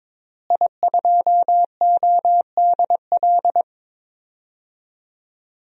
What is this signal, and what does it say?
Morse code, heard as a keyed tone, about 22 wpm, 700 Hz. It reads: I2ODL
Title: DXCC-100 Call Signs - 5 Characters - Repeated 3x - 22wpm